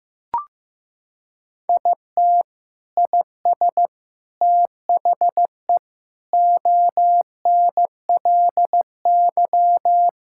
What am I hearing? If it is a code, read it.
Morse code, heard as a keyed tone, about 16 wpm, 700 Hz. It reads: IT IS THE ONLY
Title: Sentences from Top 1000 Words - Review Part 5 - 15wpm